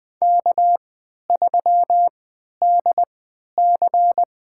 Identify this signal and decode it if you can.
Morse code, heard as a keyed tone, about 20 wpm, spaced as for 11 wpm, 700 Hz. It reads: K3DC
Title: USA Call Signs - 4 Characters - 3x Character Spacing - 20wpm